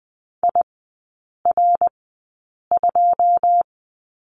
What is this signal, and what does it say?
Morse code, heard as a keyed tone, about 20 wpm, spaced as for 8 wpm, 700 Hz. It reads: IR2
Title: 081 Lesson 14e - Number 2 - Three Characters 20wpm